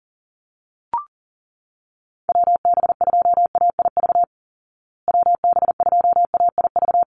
Morse code, heard as a keyed tone, about 40 wpm, 700 Hz. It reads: WB2AIV WB2AIV
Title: Call Signs Repeated 3x - 40wpm